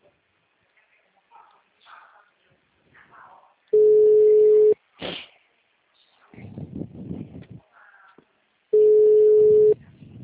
Ringback tone (Europe, 425 Hz).